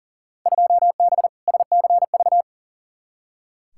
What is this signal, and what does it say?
Morse code, heard as a keyed tone, about 37 wpm, 700 Hz. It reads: 2B SCV
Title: Field Day Exchanges Encore - 40wpm